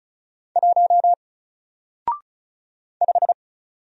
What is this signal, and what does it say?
Morse code, heard as a keyed tone, about 35 wpm, 700 Hz. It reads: 1 5
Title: Single Letter-Number Alt 35wpm